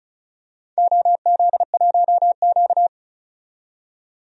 Morse code, heard as a keyed tone, about 34 wpm, 700 Hz. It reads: OZ1Q
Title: DXCC-100 Call Signs - 4 Characters - 35wpm